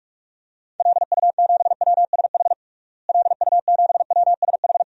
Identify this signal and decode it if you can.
Morse code, heard as a keyed tone, about 46 wpm, 700 Hz. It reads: LU7WSH LU7WSH
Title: DXCC-100 Call Signs - 6 Characters - Repeated 3x - 45wpm